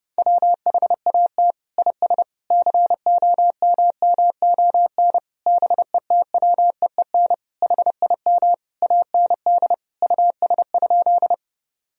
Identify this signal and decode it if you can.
Morse code, heard as a keyed tone, about 30 wpm, 700 Hz. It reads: WHAT IS COMMON BETWEEN HIM AND US?
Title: Sentences from Top 1000 Words - Review Part 3 - 30wpm